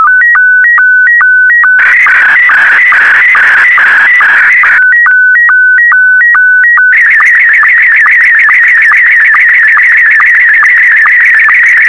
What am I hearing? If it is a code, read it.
SSTV (slow-scan television), Scottie 1.